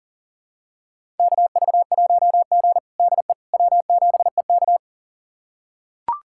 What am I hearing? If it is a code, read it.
Morse code, heard as a keyed tone, about 40 wpm, 700 Hz. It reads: KV1G DE W7EK
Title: USA Callsign DE Callsign - 4 Characters - 40wpm